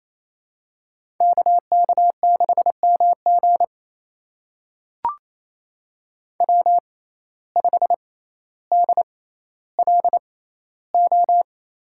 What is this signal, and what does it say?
Morse code, heard as a keyed tone, about 28 wpm, spaced as for 14 wpm, 700 Hz. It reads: KK6MG W5DLO
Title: USA Call Signs - 5 Characters - 6x Character Spacing - 28wpm